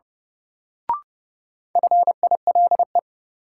Morse code, heard as a keyed tone, about 32 wpm, 700 Hz. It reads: FILE
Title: Sentences from Top 1000 Words - Review Part 4 - New Word Drill - 30wpm